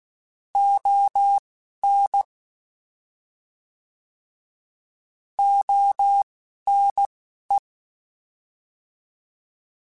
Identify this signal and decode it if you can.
Morse code, heard as a keyed tone, about 16 wpm, 800 Hz. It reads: O N O N E
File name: Alphabetical / Alphabetical mp3s / Os @10